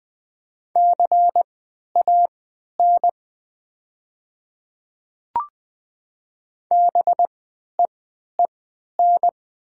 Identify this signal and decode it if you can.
Morse code, heard as a keyed tone, about 20 wpm, spaced as for 11 wpm, 700 Hz. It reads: CAN BEEN
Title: 3x Character Spacing - Top 100 Words - 20wpm